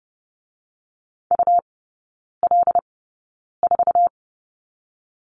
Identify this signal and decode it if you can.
Morse code, heard as a keyed tone, about 30 wpm, spaced as for 9 wpm, 690 Hz. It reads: UL4
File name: Lesson 17e - Letter u - Three Characters 30wpm